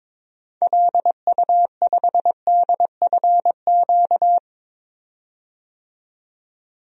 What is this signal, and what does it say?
Morse code, heard as a keyed tone, about 22 wpm, 700 Hz. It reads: LU5DFQ
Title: DXCC-100 Call Signs - 6 Characters - 22wpm